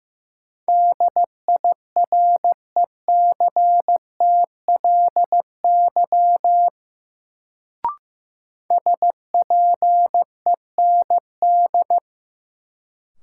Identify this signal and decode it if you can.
Morse code, heard as a keyed tone, about 15 wpm, 700 Hz. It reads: DIRECTLY SPEND
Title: Sentences from Top 800 Words - New Word Drill - 15wpm